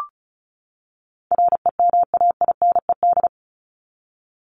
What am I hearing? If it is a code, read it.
Morse code, heard as a keyed tone, about 34 wpm, 700 Hz. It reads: REMAINED
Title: ED Suffix - Top 100 Words with Base Words 35wpm